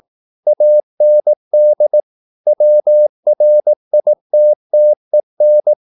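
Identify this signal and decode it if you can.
Morse code, heard as a keyed tone, about 18 wpm, 600 Hz. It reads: AND WRITTEN
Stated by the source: Morse practice at 18 wpm, English text, from CW med Gnister.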